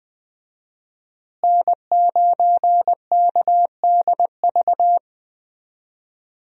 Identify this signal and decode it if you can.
Morse code, heard as a keyed tone, about 20 wpm, 700 Hz. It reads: N9KDV
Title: USA Call Signs - 5 Characters - Repeated 2x - 20wpm